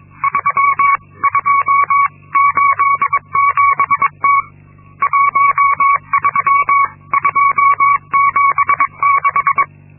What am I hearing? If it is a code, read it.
Morse code, heard as a keyed tone, about 22 wpm, 1120 Hz. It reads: 3287T 13276
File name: Linksmakalnis-2